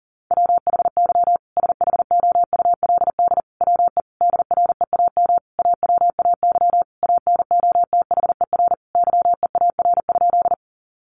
Morse code, heard as a keyed tone, about 37 wpm, 700 Hz. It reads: WHY SHOULD WE DREAM AWAY ANOTHER YEAR?
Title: Sentences from Top 1500 Words 40wpm